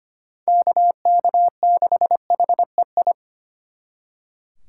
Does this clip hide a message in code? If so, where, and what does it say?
Yes: Morse code from 0.5 to 3.1 s, heard as a keyed tone, about 32 wpm, 700 Hz. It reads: KK6HEI